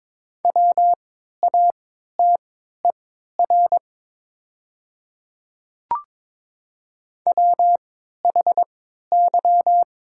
Morse code, heard as a keyed tone, about 22 wpm, spaced as for 12 wpm, 700 Hz. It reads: WATER WHY
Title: 3x Character Spacing - Top 200 Words - 22wpm